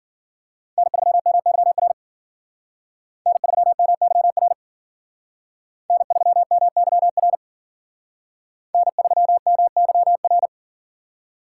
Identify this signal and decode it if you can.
Morse code, heard as a keyed tone, about 51 wpm, 700 Hz. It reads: N3MYR N3MYR N3MYR N3MYR